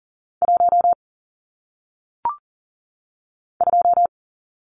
Morse code, heard as a keyed tone, about 42 wpm, 700 Hz. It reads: 1 2
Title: Numbers - 1-Digit 40wpm